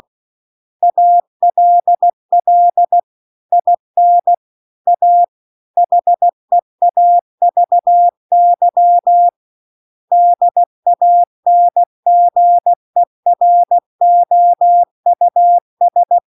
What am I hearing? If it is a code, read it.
Morse code, heard as a keyed tone, about 16 wpm, 700 Hz. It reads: ALL IN A HEAVY DANGEROUS